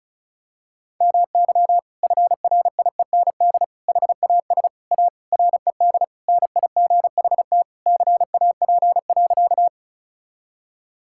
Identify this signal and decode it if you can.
Morse code, heard as a keyed tone, about 36 wpm, 700 Hz. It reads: MY FRIEND HAS A RED NIGHT CAP.
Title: Sentences from Top 1600 Words 35wpm